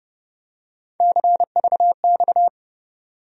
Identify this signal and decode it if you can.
Morse code, heard as a keyed tone, about 30 wpm, 700 Hz. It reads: CVX
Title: USA Call Sign Postfixes - 3 Letters 30wpm